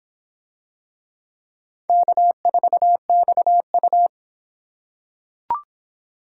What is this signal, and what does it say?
Morse code, heard as a keyed tone, about 23 wpm, 700 Hz. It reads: K4XU